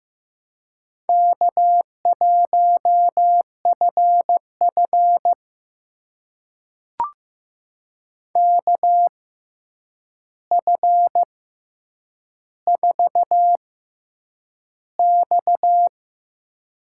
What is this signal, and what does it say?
Morse code, heard as a keyed tone, about 15 wpm, spaced as for 8 wpm, 700 Hz. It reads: K1FF KF4X
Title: USA Call Signs - 4 Characters - 6x Character Spacing - 15wpm